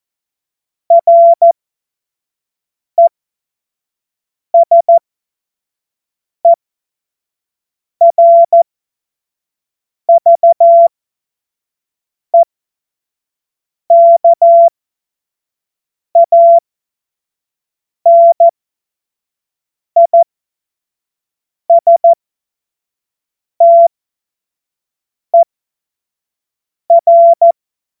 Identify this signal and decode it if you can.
Morse code, heard as a keyed tone, about 14 wpm, spaced as for 5 wpm, 680 Hz. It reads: RESERVEKANISTER
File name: Morsekurs / Witze / Witz 07. März 2023 25 F70